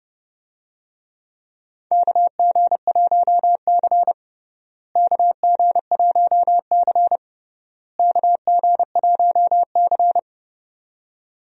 Morse code, heard as a keyed tone, about 30 wpm, 700 Hz. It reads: KG1C KG1C KG1C
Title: USA Call Signs - 4 Characters - Repeated 3x - 30wpm